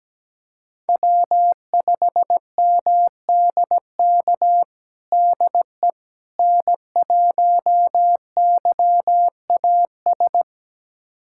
Morse code, heard as a keyed tone, about 17 wpm, 700 Hz. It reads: W5MDK DE N1YAS
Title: USA Callsign DE Callsign - 5 Characters - 17wpm